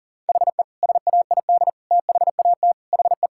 Morse code, heard as a keyed tone, about 40 wpm, 700 Hz. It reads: HE SAID THAT HE
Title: Sentences from Top 1000 Words - Review Part 6 - 40wpm